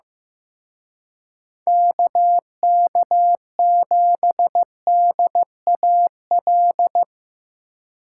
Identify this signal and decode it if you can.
Morse code, heard as a keyed tone, about 17 wpm, 700 Hz. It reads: KK7DAL
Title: Speed Racing - USA 2x3 Call Signs 15wpm